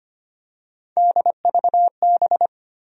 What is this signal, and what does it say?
Morse code, heard as a keyed tone, about 25 wpm, 700 Hz. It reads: DVB
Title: USA Call Sign Postfixes - 3 Letters 25wpm